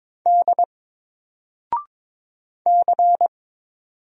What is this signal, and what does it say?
Morse code, heard as a keyed tone, about 22 wpm, 700 Hz. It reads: D C
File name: Single-Letter-Number-22wpm